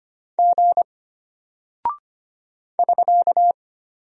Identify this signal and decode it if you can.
Morse code, heard as a keyed tone, about 25 wpm, 700 Hz. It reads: G <SK>
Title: Single Character (Letter:Number:Punctuation:Prosign) Alt 25wpm